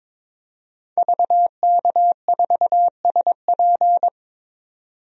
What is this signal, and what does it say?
Morse code, heard as a keyed tone, about 22 wpm, 700 Hz. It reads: VK4SP
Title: DXCC-100 Call Signs - 5 Characters - 22wpm